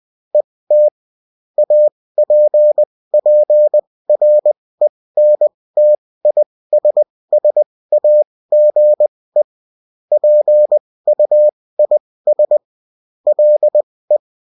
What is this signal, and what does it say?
Morse code, heard as a keyed tone, about 20 wpm, spaced as for 16 wpm, 600 Hz. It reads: ET APPRENTISSAGE PUIS LE
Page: bulletin-uft-sous-forme-texte-et-en-cw